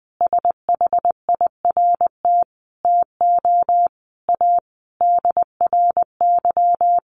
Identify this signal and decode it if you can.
Morse code, heard as a keyed tone, about 20 wpm, 700 Hz. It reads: SHIRT TO A DRY
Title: Additional Easy Phrases 20wpm